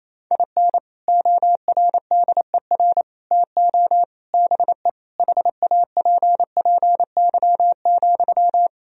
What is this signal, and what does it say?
Morse code, heard as a keyed tone, about 27 wpm, 700 Hz. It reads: IN ORDER TO BE HAPPY,
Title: Sentences from Top 1000 Words - Review Part 2 - 28wpm